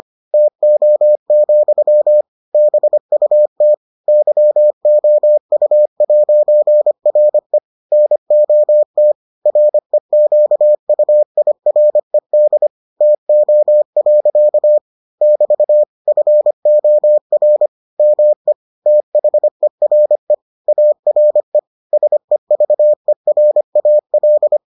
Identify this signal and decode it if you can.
Morse code, heard as a keyed tone, about 25 wpm, 600 Hz. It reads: TO, BUT YOU'RE NOT REQUIRED TO. <BT> FOR ME THERE ARE SEVERAL